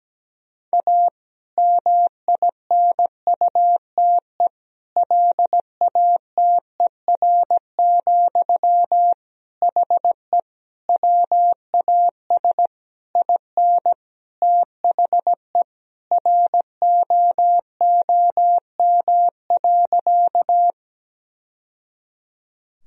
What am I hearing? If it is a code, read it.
Morse code, heard as a keyed tone, about 17 wpm, 700 Hz. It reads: A MINUTE LATER, HE WAS IN THE ROOM.
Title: Sentences from Top 1000 Words - Review Part 4 - 17wpm